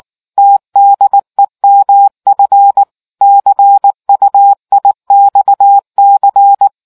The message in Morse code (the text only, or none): TDEMF CUIXC